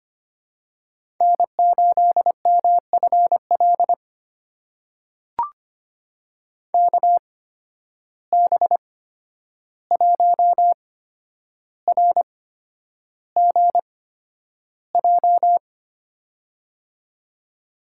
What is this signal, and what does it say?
Morse code, heard as a keyed tone, about 25 wpm, spaced as for 10 wpm, 700 Hz. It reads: N8MFL KB1RGJ
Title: USA Call Signs - 8x Character Spacing - 25wpm